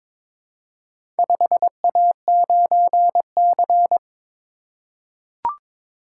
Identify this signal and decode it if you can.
Morse code, heard as a keyed tone, about 22 wpm, 700 Hz. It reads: 5A9C
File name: practice / 22wpm / Four Letter-Number Combinations - 22wpm